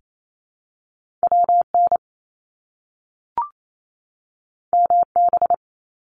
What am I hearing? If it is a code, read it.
Morse code, heard as a keyed tone, about 28 wpm, 700 Hz. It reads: WN MB
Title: Top Two Letter Combinations in Words 28wpm